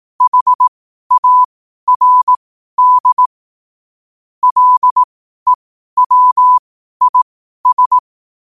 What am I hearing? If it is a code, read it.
Morse code, heard as a keyed tone, about 18 wpm, spaced as for 12 wpm, 1000 Hz. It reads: HARD LEWIS
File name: qotdfiles / 12wpm